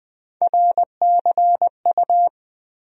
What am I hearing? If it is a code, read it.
Morse code, heard as a keyed tone, about 20 wpm, 700 Hz. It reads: RCU
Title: USA Call Sign Postfixes - 3 Letters 20wpm